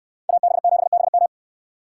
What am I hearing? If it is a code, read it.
Morse code, heard as a keyed tone, about 68 wpm, 700 Hz. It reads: AB7BM